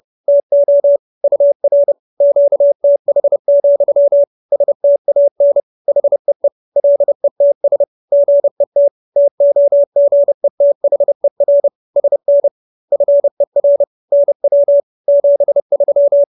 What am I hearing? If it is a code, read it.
Morse code, heard as a keyed tone, about 30 wpm, 570 Hz. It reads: TO UR QTH, STAN HEE LETS GET TOGETHER SN FER NW 73
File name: QSO311 30